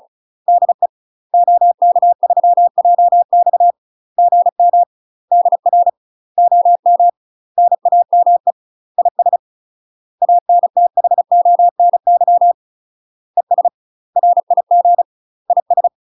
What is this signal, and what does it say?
Morse code, heard as a keyed tone, about 34 wpm, 700 Hz. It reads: DE OK3JX GM DR OM NAME IS ANTHONY ES RIG IS
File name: UK MASTER QSOs TRAINING / 35WPM / UK MASTER QSOs 0022